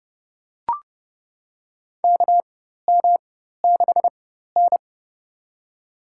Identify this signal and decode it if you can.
Morse code, heard as a keyed tone, about 32 wpm, spaced as for 14 wpm, 700 Hz. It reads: KM6N
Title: USA Call Signs - 4x Character Spacing - 30wpm